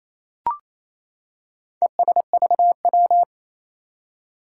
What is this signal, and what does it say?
Morse code, heard as a keyed tone, about 27 wpm, 700 Hz. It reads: ESVW
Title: Four Challenging Letter-Number Combinations 28wpm